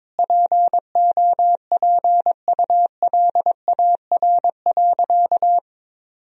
Morse code, heard as a keyed tone, about 22 wpm, 700 Hz. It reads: POPULAR.
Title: Sentences from Top 1000 Words - Review Part 3 - 22wpm